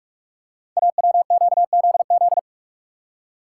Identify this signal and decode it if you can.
Morse code, heard as a keyed tone, about 44 wpm, 700 Hz. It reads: AWQZZ
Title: Code Groups - 5 Letters - 45wpm